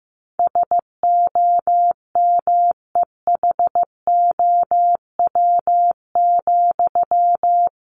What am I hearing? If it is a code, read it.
Morse code, heard as a keyed tone, about 15 wpm, 700 Hz. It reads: SOMEHOW,
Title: A Sound Of Thunder 15wpm